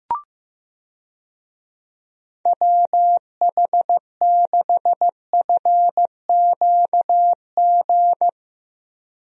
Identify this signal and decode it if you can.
Morse code, heard as a keyed tone, about 15 wpm, 700 Hz. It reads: WH6FQG
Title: USA Call Signs - 6 Characters - Repeated 3x - 15wpm